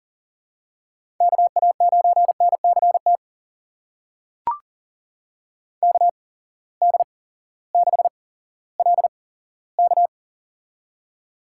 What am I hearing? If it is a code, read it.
Morse code, heard as a keyed tone, about 40 wpm, spaced as for 18 wpm, 700 Hz. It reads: KA9NCT KD6LK
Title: USA Call Signs - 8x Character Spacing - 40wpm